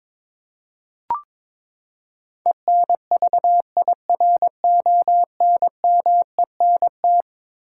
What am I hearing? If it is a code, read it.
Morse code, heard as a keyed tone, about 22 wpm, 700 Hz. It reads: ENVIRONMENT
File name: practice / 22wpm / Eleven Letter Words - 22wpm